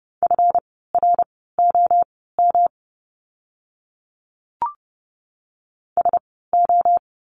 Morse code, heard as a keyed tone, about 30 wpm, spaced as for 17 wpm, 700 Hz. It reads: FROM SO
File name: course / 30:10wpm / 230 Lesson 37f - Letter x - Words 30wpm